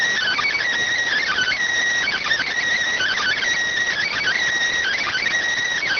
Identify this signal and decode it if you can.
Emergency Alert System (SAME) tones, heard as harsh buzzing bursts. It is heard from 1.2 to 5.2 s.